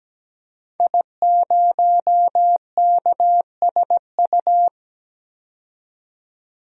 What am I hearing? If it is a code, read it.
Morse code, heard as a keyed tone, about 17 wpm, 700 Hz. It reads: I0KSU
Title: DXCC-100 Call Signs - 5 Characters - 17wpm